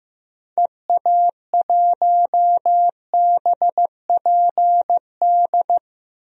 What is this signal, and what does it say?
Morse code, heard as a keyed tone, about 15 wpm, 700 Hz. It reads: EA1BPD
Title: DXCC-100 Call Signs - 6 Characters - Repeated 2x - 15wpm